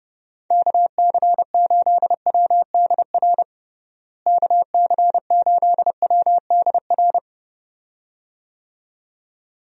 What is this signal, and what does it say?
Morse code, heard as a keyed tone, about 30 wpm, 700 Hz. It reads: KC8WDR KC8WDR
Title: USA Call Signs - 6 Characters - Repeated 2x - 30wpm